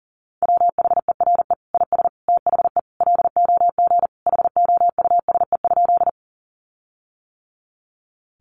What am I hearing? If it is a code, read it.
Morse code, heard as a keyed tone, about 40 wpm, 700 Hz. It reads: WHERE IS THE LOG HOUSE?
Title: Sentences from Top 1400 Words 40wpm